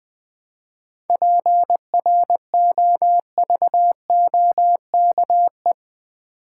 Morse code, heard as a keyed tone, about 20 wpm, 700 Hz. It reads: PROVOKE
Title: 7 Letter Words - First 3 Letters Given - 20wpm